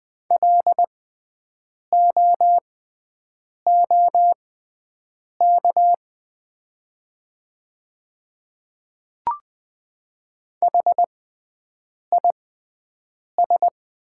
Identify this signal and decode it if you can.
Morse code, heard as a keyed tone, about 20 wpm, spaced as for 7 wpm, 700 Hz. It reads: LOOK HIS
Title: 6x Character Spacing - Top 200 Words - 20wpm